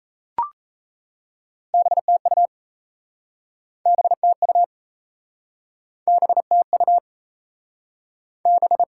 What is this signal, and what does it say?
Morse code, heard as a keyed tone, about 38 wpm, 700 Hz. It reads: BTU BTU BTU B